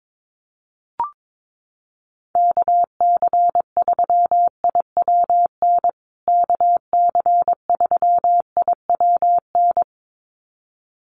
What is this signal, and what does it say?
Morse code, heard as a keyed tone, about 22 wpm, 700 Hz. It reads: KC3IWN KC3IWN
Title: USA Call Signs Repeated 2x 22wpm